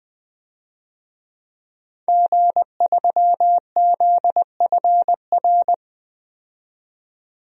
Morse code, heard as a keyed tone, about 20 wpm, 700 Hz. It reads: G3ZFR
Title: DXCC-100 Call Signs - 5 Characters - 20wpm